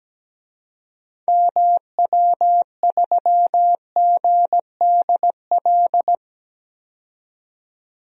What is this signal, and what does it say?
Morse code, heard as a keyed tone, about 17 wpm, 700 Hz. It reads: MW3GDL
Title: DXCC-100 Call Signs - 6 Characters - 17wpm